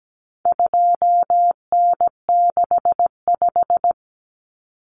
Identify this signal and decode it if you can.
Morse code, heard as a keyed tone, about 17 wpm, 700 Hz. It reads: 2N65
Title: Common Cut Numbers - 4-Digit 17wpm